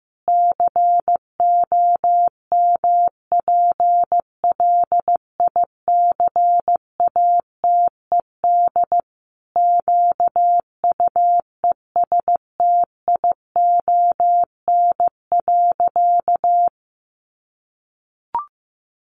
Morse code, heard as a keyed tone, about 15 wpm, 700 Hz. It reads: COMPLICATED QUESTION.
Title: Sentences from Top 1200 Words 15wpm